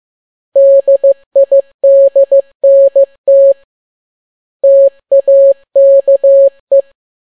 Morse code, heard as a keyed tone, about 15 wpm, 550 Hz. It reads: DIDNT TAKE
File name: n6tt p1 15C13